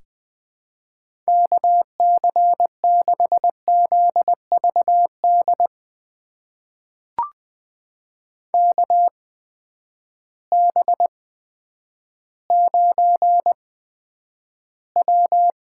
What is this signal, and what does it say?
Morse code, heard as a keyed tone, about 20 wpm, spaced as for 10 wpm, 700 Hz. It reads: KC6ZVD KB9W
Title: USA Call Signs - 6 Characters - 8x Character Spacing - 20wpm